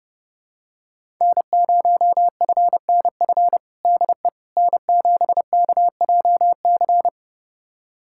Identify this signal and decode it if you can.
Morse code, heard as a keyed tone, about 30 wpm, 700 Hz. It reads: N0FNF DE N7KJC
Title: USA Callsign DE Callsign - 5 Characters - 30wpm